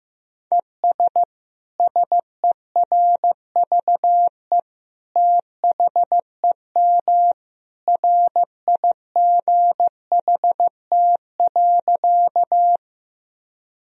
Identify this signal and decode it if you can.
Morse code, heard as a keyed tone, about 15 wpm, 700 Hz. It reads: ES SERVE THEM RIGHT.
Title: Sentences from Top 400 Words and Abbreviations 15wpm